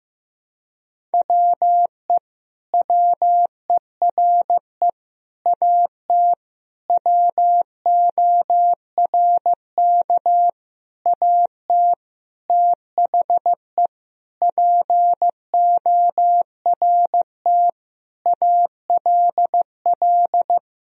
Morse code, heard as a keyed tone, about 15 wpm, 700 Hz. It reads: WE WERE AT WORK AT THE PORT ALL
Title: Sentences from Top 1000 Words - Review Part 1 - 15wpm